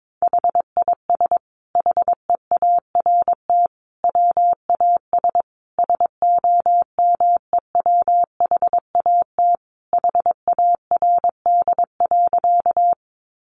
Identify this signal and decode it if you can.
Morse code, heard as a keyed tone, about 22 wpm, 700 Hz. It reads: HIS HEART WAS SOMEWHAT HARD.
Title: Sentences from Top 1300 Words 22wpm